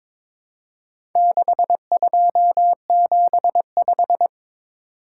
Morse code, heard as a keyed tone, about 23 wpm, 700 Hz. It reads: 6275
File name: practice / 22wpm / Four Letter-Number Combinations - 22wpm